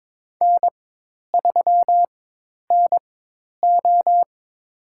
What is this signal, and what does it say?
Morse code, heard as a keyed tone, about 22 wpm, spaced as for 10 wpm, 700 Hz. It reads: N3NO
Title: USA Call Signs - 4 Characters - 4x Character Spacing - 22wpm